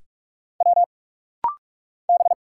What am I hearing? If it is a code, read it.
Morse code, heard as a keyed tone, about 44 wpm, 700 Hz. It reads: W B
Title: Single Letter-Number Alt - Rapid-Fire 45wpm